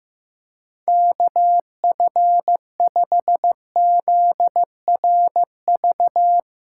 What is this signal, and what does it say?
Morse code, heard as a keyed tone, about 15 wpm, 700 Hz. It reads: KF5ZRV
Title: USA Call Signs - 8x Character Spacing - 15wpm